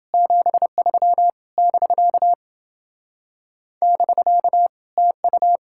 Morse code, heard as a keyed tone, about 29 wpm, 700 Hz. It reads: 73 <BK> <BK> TU